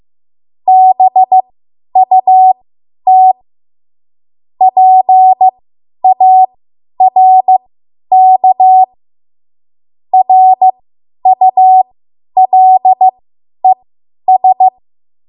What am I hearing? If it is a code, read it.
Morse code, heard as a keyed tone, about 15 wpm, spaced as for 10 wpm, 750 Hz. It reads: BUT PARK RULES
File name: Archive / 10 WPM / 10WPM